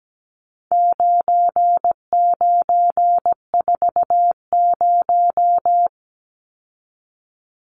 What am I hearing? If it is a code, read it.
Morse code, heard as a keyed tone, about 17 wpm, 700 Hz. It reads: 9940
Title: Numbers - 4-Digit 17wpm